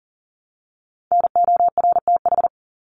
Morse code, heard as a keyed tone, about 40 wpm, 700 Hz. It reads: NORTH
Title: Top 300 Words Repeated 40wpm